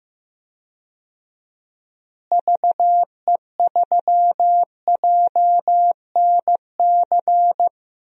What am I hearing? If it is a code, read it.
Morse code, heard as a keyed tone, about 15 wpm, 700 Hz. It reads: VE3JNC
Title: DXCC-100 Call Signs - 6 Characters - Repeated 2x - 15wpm